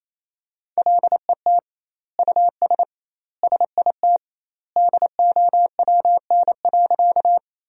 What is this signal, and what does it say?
Morse code, heard as a keyed tone, about 28 wpm, 700 Hz. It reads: LET US SIT DOWN.
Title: 2x Word-Spacing - Sentences from Top 500 Words - 28wpm